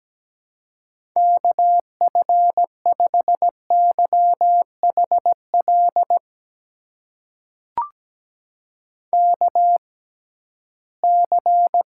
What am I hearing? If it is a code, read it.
Morse code, heard as a keyed tone, about 17 wpm, spaced as for 13 wpm, 700 Hz. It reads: KF5YHL KC
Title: USA Call Signs - 6 Characters - 6x Character Spacing - 17wpm